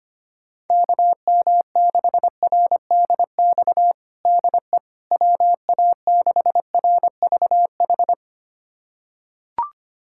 Morse code, heard as a keyed tone, about 25 wpm, 700 Hz. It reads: KM6RDX DE WA6RVH
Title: USA Callsign DE Callsign - 6 Characters - 25wpm